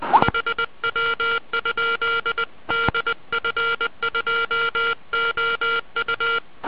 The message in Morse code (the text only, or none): HW? DF2OU